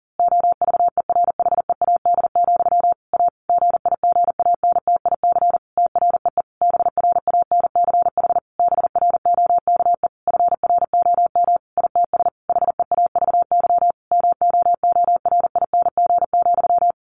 Morse code, heard as a keyed tone, about 40 wpm, 700 Hz. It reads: OVERHEAD, A GIGANTIC TREE BRANCH BROKE FROM ITS HEAVY MOORING,
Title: A Sound Of Thunder 40wpm